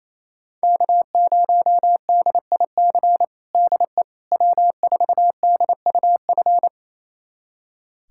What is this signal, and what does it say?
Morse code, heard as a keyed tone, about 28 wpm, 700 Hz. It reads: K0DIC DE W4DUF
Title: USA Callsign DE Callsign - 5 Characters - 28wpm